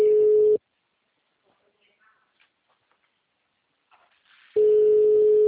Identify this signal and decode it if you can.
Ringback tone (Europe, 425 Hz), with speech.
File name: out- -2002- - - .44153